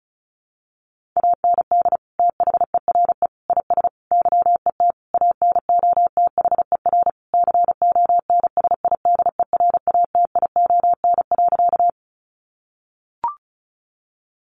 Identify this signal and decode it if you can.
Morse code, heard as a keyed tone, about 34 wpm, 700 Hz. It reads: AND THERE IS YET ANOTHER CONSIDERATION.
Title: Sentences from Top 1500 Words 35wpm